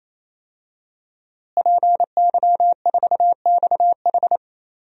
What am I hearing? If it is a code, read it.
Morse code, heard as a keyed tone, about 28 wpm, 700 Hz. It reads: PY4XH
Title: DXCC-100 Call Signs - 5 Characters - Repeated 3x - 28wpm